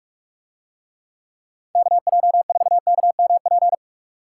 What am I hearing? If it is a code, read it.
Morse code, heard as a keyed tone, about 46 wpm, 700 Hz. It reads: KJ4KMP